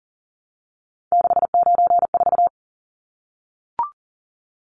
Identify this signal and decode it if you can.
Morse code, heard as a keyed tone, about 40 wpm, 700 Hz. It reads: 694
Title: Numbers - 3-Digit 40wpm